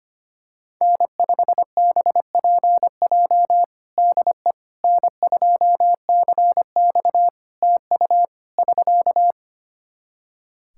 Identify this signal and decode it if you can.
Morse code, heard as a keyed tone, about 25 wpm, 700 Hz. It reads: N5BPJ DE N2CX TU <SK>
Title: K1USN SST QSO Practice 25wpm